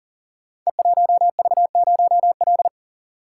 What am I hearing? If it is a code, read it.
Morse code, heard as a keyed tone, about 41 wpm, 700 Hz. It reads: E1V0L
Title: Code Groups - 5 Letter:Numbers - 40wpm